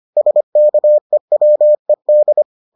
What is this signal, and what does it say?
Morse code, heard as a keyed tone, about 25 wpm, 600 Hz. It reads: SKEWED